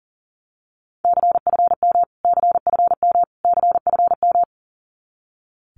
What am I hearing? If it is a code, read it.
Morse code, heard as a keyed tone, about 40 wpm, 700 Hz. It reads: CFM CFM CFM
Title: Abbreviations Repeated 3x - 40wpm